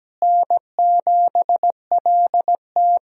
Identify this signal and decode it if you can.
Morse code, heard as a keyed tone, about 17 wpm, 700 Hz. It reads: N7LT
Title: USA Call Signs - 4 Characters - Repeated 2x - 17wpm